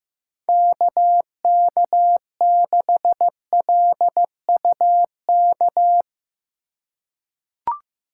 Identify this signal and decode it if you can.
Morse code, heard as a keyed tone, about 16 wpm, 700 Hz. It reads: KK6LUK
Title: USA Call Signs - 4x Character Spacing - 15wpm